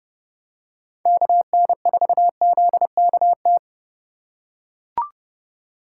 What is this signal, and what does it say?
Morse code, heard as a keyed tone, about 30 wpm, 700 Hz. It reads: KN4ZKT
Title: K1USN SST QSO Practice - 30wpm